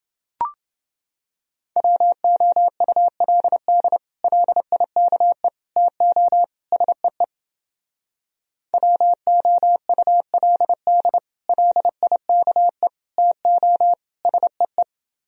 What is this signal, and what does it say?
Morse code, heard as a keyed tone, about 29 wpm, 700 Hz. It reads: WOULD LIKE TO SEE WOULD LIKE TO SEE